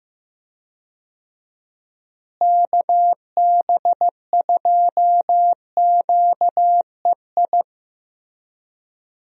Morse code, heard as a keyed tone, about 15 wpm, 700 Hz. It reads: KB2QEI
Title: USA Call Signs - 6 Characters - Repeated 2x - 15wpm